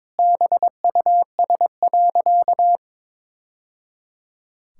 Morse code, heard as a keyed tone, about 22 wpm, 700 Hz. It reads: BUS.
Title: Sentences from Top 1000 Words - Review Part 6 - 22wpm